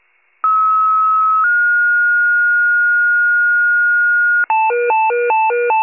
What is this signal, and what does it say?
Two-tone page, heard as a one-second tone, then a three-second tone, about 1290 then 1430 Hz. It is heard from 0.4 to 4.4 s.
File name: pager / 84 FIRE / wcalert